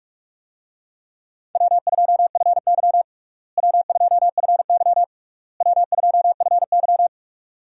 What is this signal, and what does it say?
Morse code, heard as a keyed tone, about 42 wpm, 700 Hz. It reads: W2FY W2FY W2FY
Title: USA Call Signs - 4 Characters - Repeated 3x - 45wpm